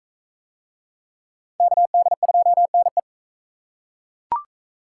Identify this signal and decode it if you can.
Morse code, heard as a keyed tone, about 39 wpm, 700 Hz. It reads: KD2NE